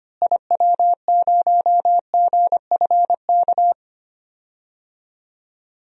Morse code, heard as a keyed tone, about 25 wpm, 700 Hz. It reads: IW0GFK
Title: DXCC-100 Call Signs - 6 Characters - Repeated 2x - 25wpm